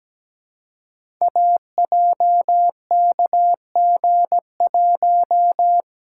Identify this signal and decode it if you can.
Morse code, heard as a keyed tone, about 17 wpm, 700 Hz. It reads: AJKG1
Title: Code Groups - 5 Letter:Numbers - 17wpm